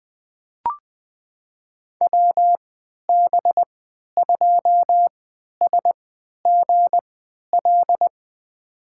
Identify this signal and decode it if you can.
Morse code, heard as a keyed tone, about 21 wpm, spaced as for 12 wpm, 700 Hz. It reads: WB2SGL
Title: USA Call Signs - 6 Characters - 3x Character Spacing - 20wpm